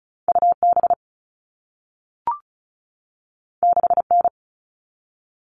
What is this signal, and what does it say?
Morse code, heard as a keyed tone, about 36 wpm, 700 Hz. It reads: UB 6N
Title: All Cut Numbers - 2-Digit 35wpm